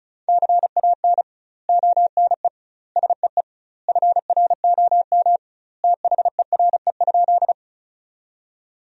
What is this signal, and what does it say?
Morse code, heard as a keyed tone, about 34 wpm, 700 Hz. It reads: CAN ONE SEE FROM THERE?
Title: 2x Word-Spacing - Sentences from Top 100 Words - 35wpm